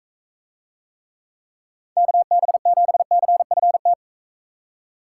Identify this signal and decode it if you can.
Morse code, heard as a keyed tone, about 42 wpm, 700 Hz. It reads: KB7CFT